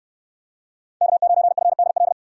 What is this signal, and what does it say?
Morse code, heard as a keyed tone, about 69 wpm, 700 Hz. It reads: K9FDL